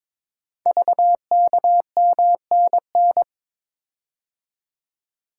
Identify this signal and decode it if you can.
Morse code, heard as a keyed tone, about 22 wpm, 700 Hz. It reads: VKMNN
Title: Code Groups - 5 Letters - 22wpm